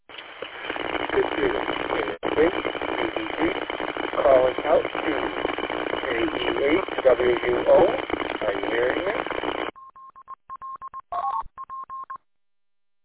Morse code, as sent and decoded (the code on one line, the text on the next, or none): none